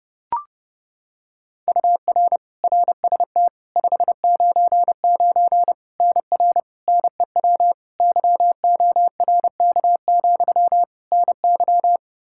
Morse code, heard as a keyed tone, about 32 wpm, 700 Hz. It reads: UR RST 599 NR NEW YORK, NY
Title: K1USN SST QSO Practice 30wpm